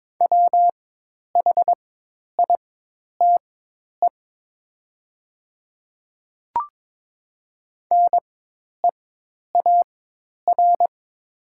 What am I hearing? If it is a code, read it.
Morse code, heard as a keyed tone, about 22 wpm, spaced as for 10 wpm, 700 Hz. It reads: WHITE NEAR
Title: 4x Character Spacing - Top 300 Words - 22wpm